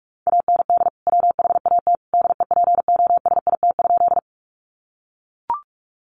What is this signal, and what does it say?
Morse code, heard as a keyed tone, about 44 wpm, 700 Hz. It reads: AND WHAT DEPOSIT?
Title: Sentences from Top 1300 Words 45wpm